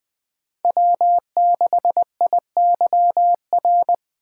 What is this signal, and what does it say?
Morse code, heard as a keyed tone, about 20 wpm, 700 Hz. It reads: W6IYR